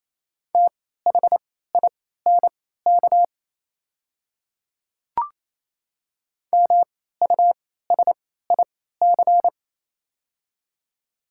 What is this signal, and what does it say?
Morse code, heard as a keyed tone, about 27 wpm, spaced as for 16 wpm, 700 Hz. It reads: THINK MUSIC
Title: 3x Character Spacing - Top 300 Words - 28wpm